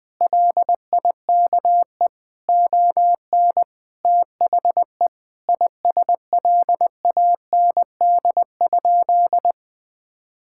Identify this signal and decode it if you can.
Morse code, heard as a keyed tone, about 20 wpm, 700 Hz. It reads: LIKE ON THE ISLAND?
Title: Sentences from Top 1600 Words 20wpm